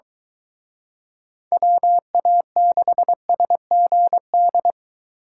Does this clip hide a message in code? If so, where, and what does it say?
Yes: Morse code from 1.5 to 4.7 s, heard as a keyed tone, about 23 wpm, 700 Hz. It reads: WA6SGD